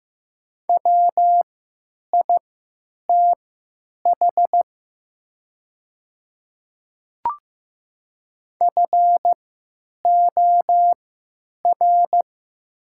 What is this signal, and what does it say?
Morse code, heard as a keyed tone, about 15 wpm, spaced as for 9 wpm, 690 Hz. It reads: WITH FOR
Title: 3x Character Spacing - Top 100 Words - 15wpm